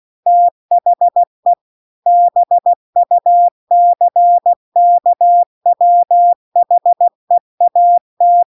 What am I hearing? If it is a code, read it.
Morse code, heard as a keyed tone, about 16 wpm, 700 Hz. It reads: THE BUCKWHEAT